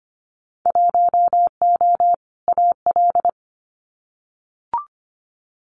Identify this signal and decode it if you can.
Morse code, heard as a keyed tone, about 25 wpm, 700 Hz. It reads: 1O AL
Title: Winter Field Day Exchanges 25wpm